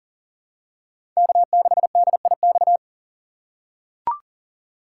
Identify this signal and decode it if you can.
Morse code, heard as a keyed tone, about 40 wpm, 700 Hz. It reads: K6DIX
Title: USA Call Signs - 5 Characters - 3x Character Spacing - 40wpm